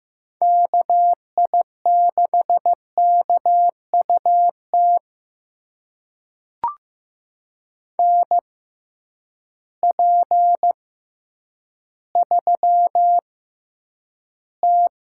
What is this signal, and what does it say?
Morse code, heard as a keyed tone, about 15 wpm, spaced as for 9 wpm, 700 Hz. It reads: KI6KUT NP3T
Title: USA Call Signs - 6x Character Spacing - 15wpm